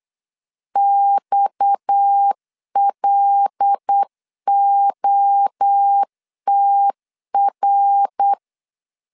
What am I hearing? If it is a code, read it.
Morse code, heard as a keyed tone, about 8 wpm, 800 Hz. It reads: XLOTR
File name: LEZIONI / 6 PKX0 / 6 PKX 8wpm 3041